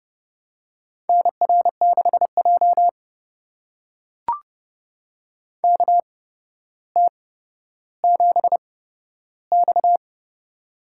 Morse code, heard as a keyed tone, about 32 wpm, spaced as for 13 wpm, 700 Hz. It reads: NR6J KT7X
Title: USA Call Signs - 4 Characters - 8x Character Spacing - 30wpm